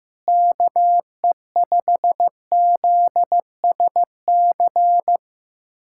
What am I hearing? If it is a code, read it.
Morse code, heard as a keyed tone, about 15 wpm, 700 Hz. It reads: KE5ZSC
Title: USA Callsign DE Callsign 15wpm